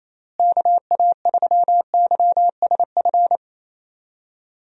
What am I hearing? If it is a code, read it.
Morse code, heard as a keyed tone, about 28 wpm, 700 Hz. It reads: KA3YSF
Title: USA Call Signs - 6 Characters - Repeated 3x - 28wpm